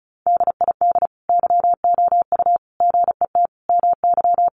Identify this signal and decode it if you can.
Morse code, heard as a keyed tone, about 34 wpm, 700 Hz. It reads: DID YOU GET MY
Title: Sentences from Top 1100 Words 35wpm